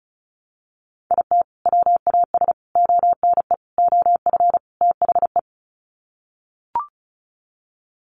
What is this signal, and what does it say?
Morse code, heard as a keyed tone, about 34 wpm, 700 Hz. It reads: IT WAS ONE OF THE
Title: Sets of 5 Words 35wpm